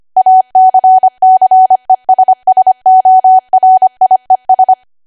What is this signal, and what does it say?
Morse code, heard as a keyed tone, about 25 wpm, 750 Hz. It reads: ACCESSORIES
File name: Archive / 25 WPM / 25WPM